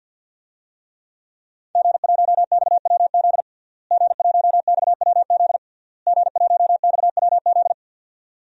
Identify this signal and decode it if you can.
Morse code, heard as a keyed tone, about 51 wpm, 700 Hz. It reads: G1XWZ G1XWZ G1XWZ
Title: DXCC-100 Call Signs - 5 Characters - Repeated 3x - 50wpm